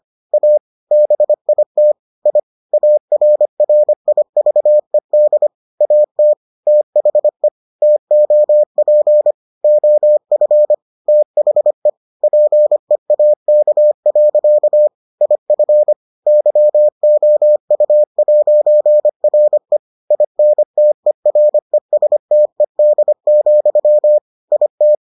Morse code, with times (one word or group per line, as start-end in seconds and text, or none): 0.3-0.6 A
0.9-1.9 BIT
2.3-2.4 I
2.7-5.5 ARRIVED
5.8-6.3 AT
6.7-7.5 THE
7.8-9.3 TOP
9.6-10.7 OF
11.1-11.9 THE
12.2-14.9 PEAK.
15.2-15.9 IF
16.3-19.8 YOU'RE
20.1-24.2 INTERESTED,
24.5-24.9 IT